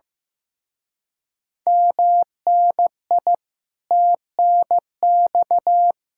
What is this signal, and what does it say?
Morse code, heard as a keyed tone, about 15 wpm, 700 Hz. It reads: MNI TNX
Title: Speed-Racing Common QSO Phrases Encore - 15wpm